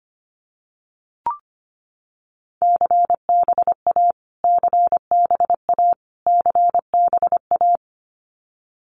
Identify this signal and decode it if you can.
Morse code, heard as a keyed tone, about 25 wpm, 700 Hz. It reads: CBA CBA CBA
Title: Abbreviations Repeated 3x - 25wpm